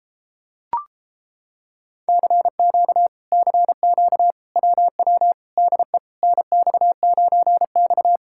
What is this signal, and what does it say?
Morse code, heard as a keyed tone, about 33 wpm, 700 Hz. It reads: CQ CQ WW DE NX9X